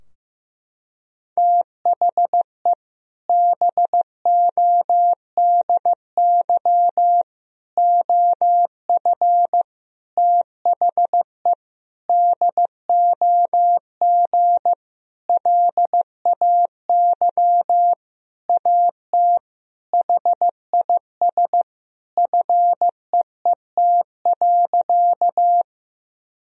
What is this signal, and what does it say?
Morse code, heard as a keyed tone, about 15 wpm, 700 Hz. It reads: THE BODY OF THE DOG LAY AT HIS FEET.
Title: Sentences from Top 1000 Words - Review Part 2 - 15wpm